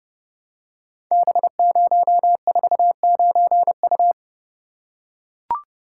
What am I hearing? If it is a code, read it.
Morse code, heard as a keyed tone, about 30 wpm, 700 Hz. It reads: B049U
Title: Code Groups - 5 Letter:Numbers - 30wpm